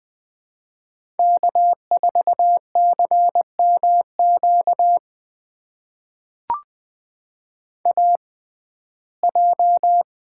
Morse code, heard as a keyed tone, about 20 wpm, spaced as for 14 wpm, 700 Hz. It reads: K4CMQ AJ
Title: USA Call Signs - 5 Characters - 6x Character Spacing - 20wpm